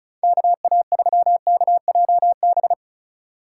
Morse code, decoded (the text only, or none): KA3KJB